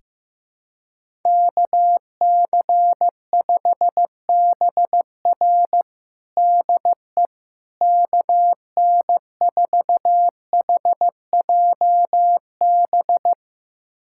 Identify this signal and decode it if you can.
Morse code, heard as a keyed tone, about 15 wpm, 700 Hz. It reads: KC5BR DE KN4HJB
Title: USA Callsign DE Callsign 15wpm